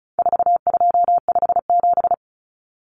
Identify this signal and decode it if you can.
Morse code, heard as a keyed tone, about 34 wpm, 700 Hz. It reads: 4257
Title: Numbers - 4-Digit 35wpm